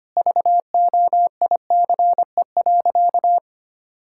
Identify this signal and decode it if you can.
Morse code, heard as a keyed tone, about 25 wpm, 700 Hz. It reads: VOICE.
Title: Sentences from Top 1600 Words 25wpm